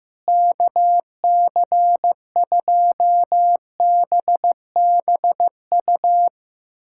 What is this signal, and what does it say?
Morse code, heard as a keyed tone, about 15 wpm, 700 Hz. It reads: KC2BBU
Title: K1USN SST QSO Practice - 15wpm